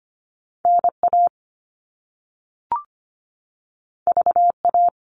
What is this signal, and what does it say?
Morse code, heard as a keyed tone, about 25 wpm, 700 Hz. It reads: N A V A
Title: All Cut Numbers - 2-Digit 25wpm